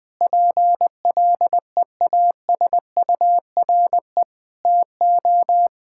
Morse code, heard as a keyed tone, about 20 wpm, 700 Hz. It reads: PLEASURE TO
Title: Sentences from Top 1600 Words 20wpm